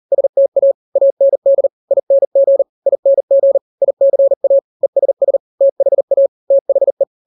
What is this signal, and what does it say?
Morse code, heard as a keyed tone, about 40 wpm, 550 Hz. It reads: STA AND ING ING ICA ESS THA THE
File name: n-grams / 3-grams 40 35 WPM